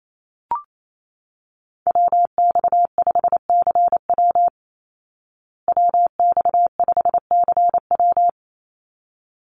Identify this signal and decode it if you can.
Morse code, heard as a keyed tone, about 28 wpm, 700 Hz. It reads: WX5CW WX5CW
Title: Call Signs Repeated 3x - 28wpm